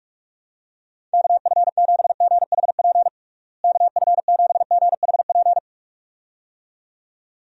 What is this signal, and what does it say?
Morse code, heard as a keyed tone, about 46 wpm, 700 Hz. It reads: KF7GHP KF7GHP
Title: USA Call Signs - 6 Characters - Repeated 2x - 45wpm